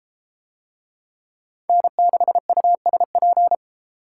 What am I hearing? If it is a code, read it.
Morse code, heard as a keyed tone, about 31 wpm, 700 Hz. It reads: N6USP